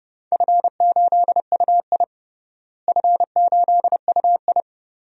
Morse code, heard as a keyed tone, about 32 wpm, 700 Hz. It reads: F8UI F8UI
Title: DXCC-100 Call Signs - 4 Characters - Repeated 2x - 30wpm